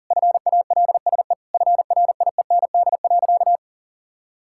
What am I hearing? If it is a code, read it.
Morse code, heard as a keyed tone, about 40 wpm, 700 Hz. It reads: FALSE FRIEND.
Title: Sentences from Top 1600 Words 40wpm